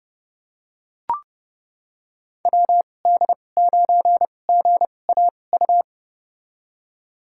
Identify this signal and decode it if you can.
Morse code, heard as a keyed tone, about 30 wpm, spaced as for 22 wpm, 700 Hz. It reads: WD9GAU
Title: USA Call Signs - 6 Characters - 2x Character Spacing - 30wpm